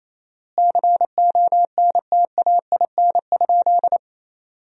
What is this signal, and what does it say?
Morse code, heard as a keyed tone, about 28 wpm, 700 Hz. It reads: CONTAIN?
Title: 4x Word-Spacing - Sentences from Top 500 Words - 28wpm